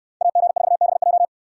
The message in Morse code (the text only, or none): AG4BJ